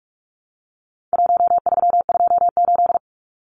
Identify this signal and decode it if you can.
Morse code, heard as a keyed tone, about 44 wpm, 700 Hz. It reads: 1328
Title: Numbers - 4-Digit 45wpm